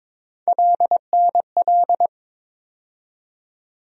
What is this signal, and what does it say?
Morse code, heard as a keyed tone, about 22 wpm, 700 Hz. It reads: LNL
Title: USA Call Sign Postfixes - 3 Letters 22wpm